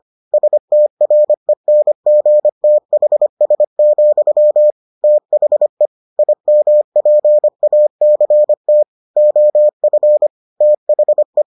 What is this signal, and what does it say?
Morse code, heard as a keyed tone, about 25 wpm, 600 Hz. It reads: STRENGTHS, THE IMPACT OF THE